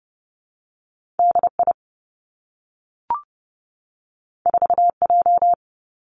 Morse code, heard as a keyed tone, about 30 wpm, 700 Hz. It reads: D I 4 J
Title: Two Challenging Letter-Number Combinations 30wpm